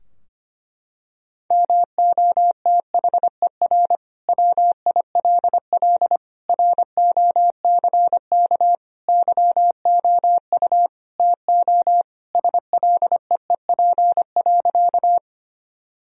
Morse code, heard as a keyed tone, about 25 wpm, 700 Hz. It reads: MOTHER WILL ROCK YOU TO SLEEP.
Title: Sentences from Top 1000 Words - Review Part 2 - 25wpm